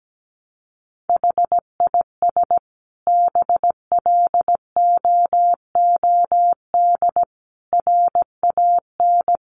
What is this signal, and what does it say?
Morse code, heard as a keyed tone, about 17 wpm, 700 Hz. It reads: HIS BLOOD RAN
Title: Sentences from Top 1100 Words 17wpm